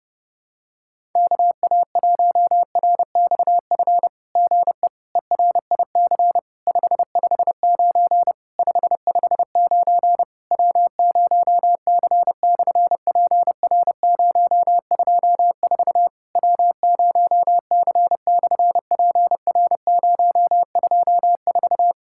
Morse code, heard as a keyed tone, about 30 wpm, 700 Hz. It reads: KA1RXF GE ERIC 559 559 W0C/PR024 W0C/PR024
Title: SOTA QSO Practice 30wpm